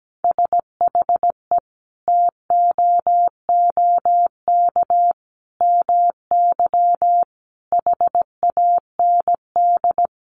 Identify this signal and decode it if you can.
Morse code, heard as a keyed tone, about 17 wpm, 700 Hz. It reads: SHE TOOK MY HAND
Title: Sentences from Top 1500 Words 17wpm